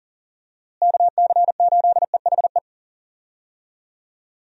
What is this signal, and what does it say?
Morse code, heard as a keyed tone, about 40 wpm, 700 Hz. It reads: KC8EHE
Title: USA Call Signs - 6 Characters - Repeated 3x - 40wpm